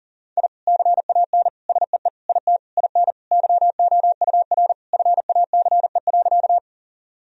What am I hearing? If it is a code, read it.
Morse code, heard as a keyed tone, about 37 wpm, 700 Hz. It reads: I CAN SEE IT IN YOUR FACE.
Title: Sentences from Top 1000 Words - Review Part 2 - 40wpm